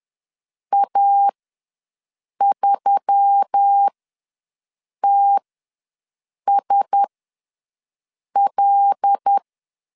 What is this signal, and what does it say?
Morse code, heard as a keyed tone, about 10 wpm, spaced as for 5 wpm, 800 Hz. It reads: A3TSL
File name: LEZIONI / 6 PKX0 / 6 PKX0 ws 10wpm 4220